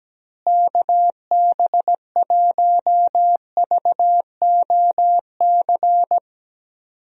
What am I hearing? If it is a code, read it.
Morse code, heard as a keyed tone, about 17 wpm, 700 Hz. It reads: KB1VOC
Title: USA Call Signs - 6 Characters - Repeated 2x - 17wpm